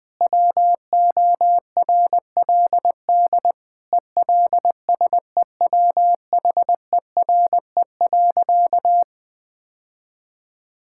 Morse code, heard as a keyed tone, about 20 wpm, 700 Hz. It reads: WORLD ELSEWHERE.
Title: Sentences from Top 1600 Words 20wpm